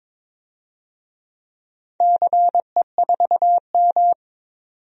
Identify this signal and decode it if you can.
Morse code, heard as a keyed tone, about 22 wpm, 700 Hz. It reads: CE4M
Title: DXCC-100 Call Signs - 4 Characters - 22wpm